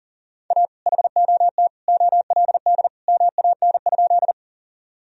Morse code, heard as a keyed tone, about 40 wpm, 700 Hz. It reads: A HOT OLD MAN?
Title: Sentences from Top 1000 Words - Review Part 3 - 40wpm